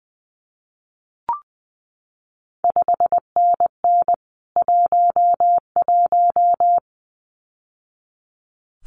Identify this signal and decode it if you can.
Morse code, heard as a keyed tone, about 20 wpm, 700 Hz. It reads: 5NN 11
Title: CQ WW DX Contest Exchange Encore - 20wpm